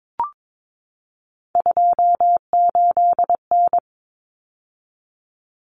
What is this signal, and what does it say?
Morse code, heard as a keyed tone, about 22 wpm, 700 Hz. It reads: 28N
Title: Common Cut Numbers - 3-Digit 22wpm